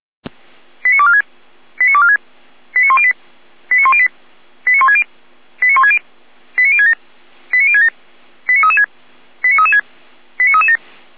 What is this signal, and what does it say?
Five-tone selective call (ZVEI), heard as quick runs of five tones, tones of about 70 ms. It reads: 89246 89246 89108 89108 89160 89160 89067 89067 89336 89336 89337